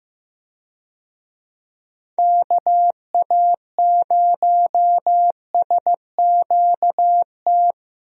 Morse code, heard as a keyed tone, about 15 wpm, 700 Hz. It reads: KA0SQT
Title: USA Call Signs - 6 Characters - 15wpm